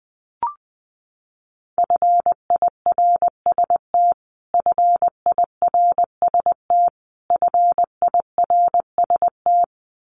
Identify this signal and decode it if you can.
Morse code, heard as a keyed tone, about 20 wpm, 700 Hz. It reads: FIRST FIRST FIRST
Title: Top 200 Words Repeated 20wpm